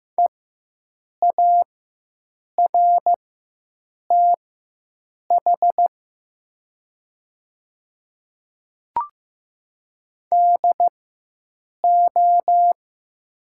Morse code, heard as a keyed tone, about 15 wpm, spaced as for 7 wpm, 690 Hz. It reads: EARTH DO
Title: 4x Character Spacing - Top 500 Words - 15wpm